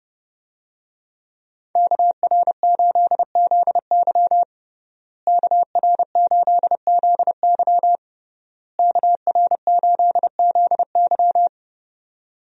Morse code, heard as a keyed tone, about 30 wpm, 700 Hz. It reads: KR8ZY KR8ZY KR8ZY
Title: USA Call Signs - 5 Characters - Repeated 3x - 30wpm